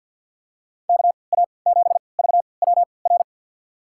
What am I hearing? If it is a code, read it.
Morse code, heard as a keyed tone, about 52 wpm, spaced as for 29 wpm, 700 Hz. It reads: KA7VWR
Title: USA Call Signs - 6 Characters - 3x Character Spacing - 50wpm